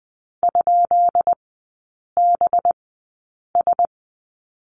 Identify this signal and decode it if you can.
Morse code, heard as a keyed tone, about 20 wpm, spaced as for 8 wpm, 690 Hz. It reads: ?BS
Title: 212 Lesson 34e - Number 8 - Three Characters 20wpm